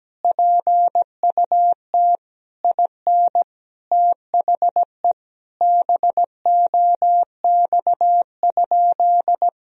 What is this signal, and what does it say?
Morse code, heard as a keyed tone, about 17 wpm, 700 Hz. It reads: PUT IN THE BOX?
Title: Sentences from Top 1000 Words - Review Part 2 - 17wpm